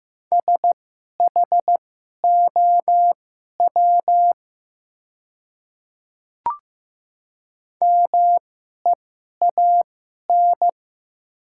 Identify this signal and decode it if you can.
Morse code, heard as a keyed tone, about 15 wpm, spaced as for 11 wpm, 700 Hz. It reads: SHOW MEAN
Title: 2x Character Spacing - Top 200 Words - 15wpm